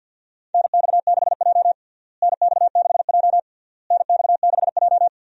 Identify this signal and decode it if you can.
Morse code, heard as a keyed tone, about 52 wpm, 700 Hz. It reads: NX6J NX6J NX6J
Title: USA Call Signs - 4 Characters - Repeated 3x - 50wpm